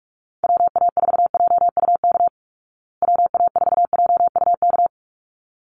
Morse code, heard as a keyed tone, about 44 wpm, 700 Hz. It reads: WA4JUK WA4JUK
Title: Call Signs Repeated 3x - 45wpm